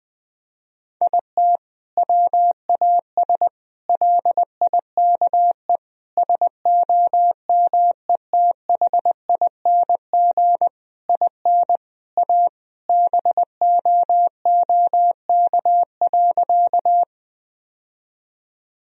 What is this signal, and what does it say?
Morse code, heard as a keyed tone, about 19 wpm, 700 Hz. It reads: IT WAS LIKE SOMETHING IN A BOOK.
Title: Sentences from Top 1000 Words - Review Part 1 - 20wpm